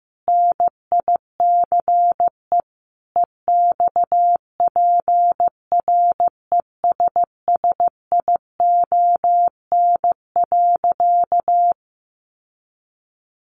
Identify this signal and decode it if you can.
Morse code, heard as a keyed tone, about 15 wpm, 700 Hz. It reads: NICE EXPRESSION.
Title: Sentences from Top 1200 Words 15wpm